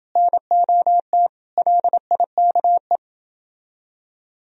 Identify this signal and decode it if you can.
Morse code, heard as a keyed tone, about 29 wpm, 700 Hz. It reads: NOT LIKE